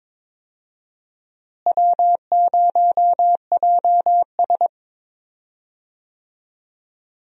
Morse code, heard as a keyed tone, about 22 wpm, 700 Hz. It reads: W0JS
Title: USA Call Signs - 4 Characters - Repeated 3x - 22wpm